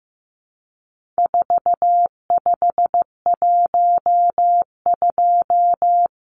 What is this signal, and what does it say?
Morse code, heard as a keyed tone, about 15 wpm, 700 Hz. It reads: 4512
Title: Numbers - 4-Digit 15wpm